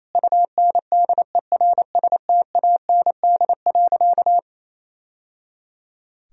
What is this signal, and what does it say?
Morse code, heard as a keyed tone, about 28 wpm, 700 Hz. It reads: UNDERSTAND.
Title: Sentences from Top 1000 Words - Review Part 1 - 28wpm